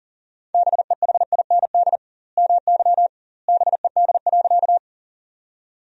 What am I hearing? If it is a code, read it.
Morse code, heard as a keyed tone, about 37 wpm, 700 Hz. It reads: BEHIND MY BED.
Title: 2x Word-Spacing - Sentences from Top 500 Words - 40wpm